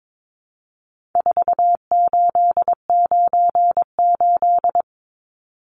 Morse code, heard as a keyed tone, about 22 wpm, 700 Hz. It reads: 4898
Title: Numbers - 4-Digit 22wpm